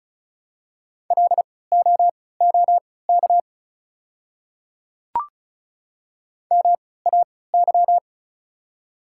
Morse code, heard as a keyed tone, about 36 wpm, spaced as for 20 wpm, 700 Hz. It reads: LOOK MAY
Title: 3x Character Spacing - Top 100 Words - 35wpm